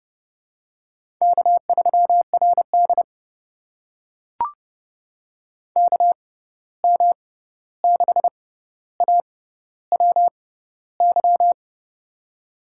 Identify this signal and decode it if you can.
Morse code, heard as a keyed tone, about 31 wpm, spaced as for 14 wpm, 700 Hz. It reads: K3RD KM6AWY
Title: USA Call Signs - 6x Character Spacing - 30wpm